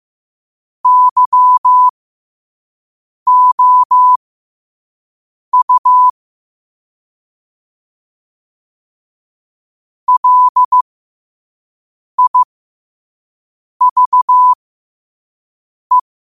Morse code, heard as a keyed tone, about 15 wpm, spaced as for 5 wpm, 1000 Hz. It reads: YOU LIVE
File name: qotdfiles / 5wpm